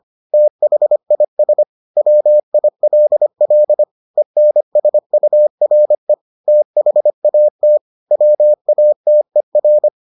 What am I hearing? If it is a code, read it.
Morse code, heard as a keyed tone, about 25 wpm, 600 Hz. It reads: THIS WILL ENSURE THAT WATER